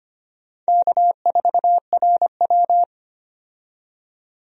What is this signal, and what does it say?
Morse code, heard as a keyed tone, about 25 wpm, 700 Hz. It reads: K4RW
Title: USA Call Signs - 4 Characters - 25wpm